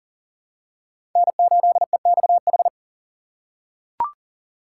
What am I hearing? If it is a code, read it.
Morse code, heard as a keyed tone, about 40 wpm, 700 Hz. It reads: N8EXH
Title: USA Call Signs - 5 Characters - 2x Character Spacing - 40wpm